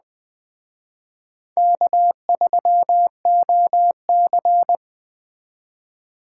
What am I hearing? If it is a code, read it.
Morse code, heard as a keyed tone, about 20 wpm, 700 Hz. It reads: K3OC
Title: Speed Racing - USA 1x2 Call Signs 17wpm